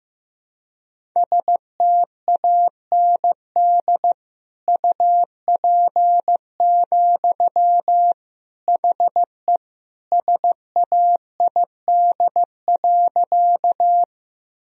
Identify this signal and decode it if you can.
Morse code, heard as a keyed tone, about 16 wpm, 700 Hz. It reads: STAND UP, HE SAID.
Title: Sentences from Top 1000 Words - Review Part 1 - 15wpm